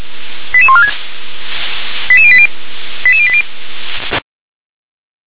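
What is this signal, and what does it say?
Five-tone selective call (ZVEI), heard as quick runs of five tones, tones of about 70 ms. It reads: 88146 88080 88080
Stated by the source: ZVEI five-tone call received by SDR.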